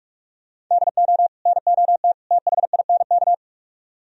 Morse code, heard as a keyed tone, about 45 wpm, 700 Hz. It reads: DO NOT THINK